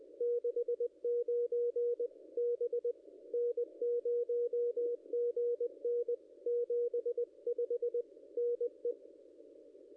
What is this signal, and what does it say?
Morse code, heard as a keyed tone, about 20 wpm, 440 Hz. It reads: 69 B N0GN 75 NE
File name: contests / ss-cw-k5tr-op-n5rz -0200